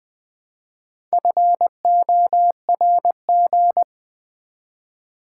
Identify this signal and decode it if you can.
Morse code, heard as a keyed tone, about 20 wpm, 700 Hz. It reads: FORG
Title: Build A Word - 1 Letter At A Time - 5-Letter Words - 20wpm